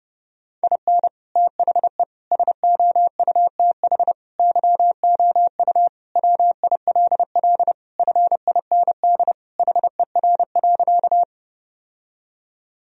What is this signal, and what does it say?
Morse code, heard as a keyed tone, about 32 wpm, 700 Hz. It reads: IN THE SOUTH YOU WILL FIND HER.
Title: Sentences from Top 1000 Words - Review Part 2 - 30wpm